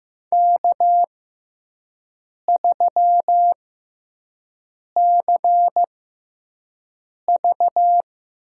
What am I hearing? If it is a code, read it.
Morse code, heard as a keyed tone, about 15 wpm, spaced as for 5 wpm, 700 Hz. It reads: K3CV
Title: USA Call Signs - 4 Characters - 6x Character Spacing - 15wpm